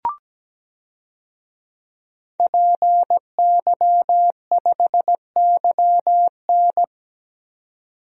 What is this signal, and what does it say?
Morse code, heard as a keyed tone, about 17 wpm, 700 Hz. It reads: PY5YN
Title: DXCC-100 Call Signs - 5 Characters - 17wpm